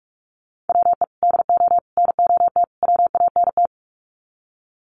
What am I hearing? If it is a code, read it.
Morse code, heard as a keyed tone, about 44 wpm, 700 Hz. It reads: WE DO NOT WANT
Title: Sets of 4 Words 45wpm